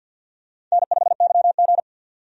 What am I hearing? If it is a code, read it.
Morse code, heard as a keyed tone, about 49 wpm, 700 Hz. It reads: N5YG
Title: USA Call Signs - 4 Characters - 50wpm